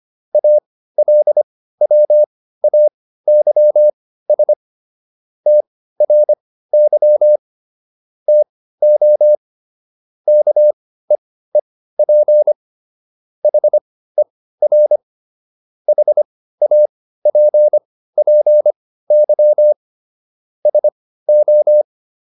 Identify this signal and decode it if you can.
Morse code, heard as a keyed tone, about 25 wpm, spaced as for 15 wpm, 600 Hz. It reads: ALWAYS TRY TO KEEP HER HAPPY SO